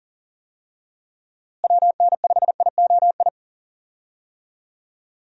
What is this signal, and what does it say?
Morse code, heard as a keyed tone, about 40 wpm, 700 Hz. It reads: WN5IOI
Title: USA Call Signs - 6 Characters - Repeated 2x - 40wpm